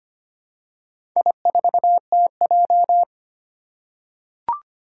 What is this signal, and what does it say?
Morse code, heard as a keyed tone, about 25 wpm, 700 Hz. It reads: I4TJ
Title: Four Challenging Letter-Number Combinations 25wpm